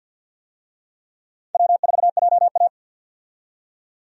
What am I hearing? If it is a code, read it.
Morse code, heard as a keyed tone, about 48 wpm, 700 Hz. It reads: W4JA
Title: USA Call Signs - 4 Characters - 50wpm